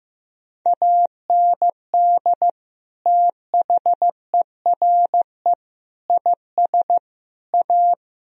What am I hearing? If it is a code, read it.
Morse code, heard as a keyed tone, about 15 wpm, 700 Hz. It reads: AND THERE IS A
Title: Sentences from Top 1000 Words - Review Part 1 - 15wpm